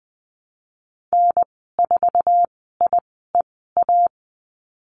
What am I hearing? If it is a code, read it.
Morse code, heard as a keyed tone, about 20 wpm, spaced as for 14 wpm, 700 Hz. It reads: N4IEA
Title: 069 Lesson 12g - Letter d - Call Signs 20wpm